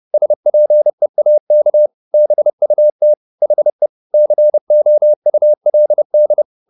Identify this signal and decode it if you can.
Morse code, heard as a keyed tone, about 30 wpm, 600 Hz. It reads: SPEAK BUT HE COULD